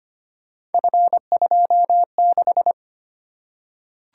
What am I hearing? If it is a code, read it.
Morse code, heard as a keyed tone, about 25 wpm, 700 Hz. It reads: F26
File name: Three Letter-Number Combinations - 25wpm